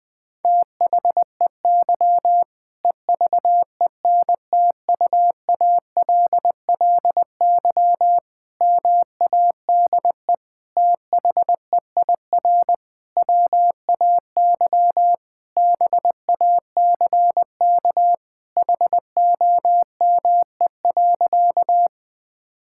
Morse code, heard as a keyed tone, about 20 wpm, 700 Hz. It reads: THEY EVENTUALLY MADE THEIR WAY BACK HOME.
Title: Sentences from Top 1000 Words - Review Part 6 - 20wpm